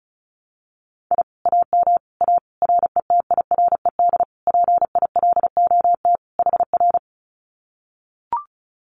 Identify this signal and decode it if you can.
Morse code, heard as a keyed tone, about 34 wpm, 700 Hz. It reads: I AM A RETIRED PILOT HR
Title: Common QSO Phrases Encore 35wpm